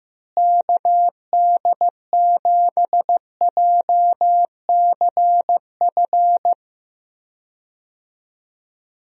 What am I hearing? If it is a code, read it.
Morse code, heard as a keyed tone, about 15 wpm, 700 Hz. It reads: KD7JCF
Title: USA Call Signs - 6 Characters - Repeated 2x - 15wpm